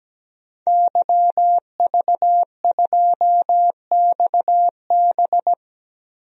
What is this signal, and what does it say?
Morse code, heard as a keyed tone, about 17 wpm, 700 Hz. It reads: YV2XB
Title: DXCC-100 Call Signs - 5 Characters - Repeated 2x - 17wpm